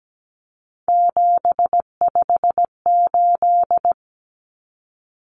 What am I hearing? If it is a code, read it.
Morse code, heard as a keyed tone, about 17 wpm, 700 Hz. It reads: 758
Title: Numbers - 3-Digit 17wpm